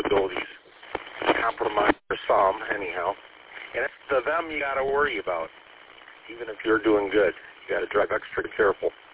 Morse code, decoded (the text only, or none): none